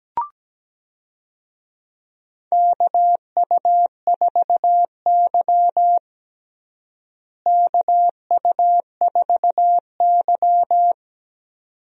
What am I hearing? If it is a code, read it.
Morse code, heard as a keyed tone, about 17 wpm, 700 Hz. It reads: KU4Y KU4Y
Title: USA Call Signs - 4 Characters - Repeated 2x - 17wpm